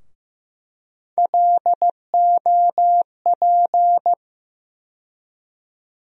Morse code, heard as a keyed tone, about 15 wpm, 700 Hz. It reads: LOP
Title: USA Call Sign Postfixes - 3 Letters 15wpm